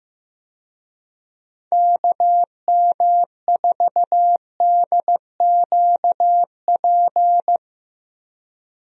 Morse code, heard as a keyed tone, about 15 wpm, 700 Hz. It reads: KM4DQP
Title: USA Call Signs - 6 Characters - Repeated 3x - 15wpm